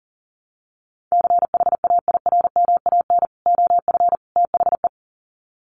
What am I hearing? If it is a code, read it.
Morse code, heard as a keyed tone, about 40 wpm, 700 Hz. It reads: CHAIRMAN OF THE
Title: Sets of 3 Words 40wpm